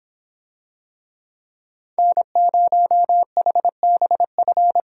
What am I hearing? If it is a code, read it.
Morse code, heard as a keyed tone, about 26 wpm, 700 Hz. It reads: N0HBF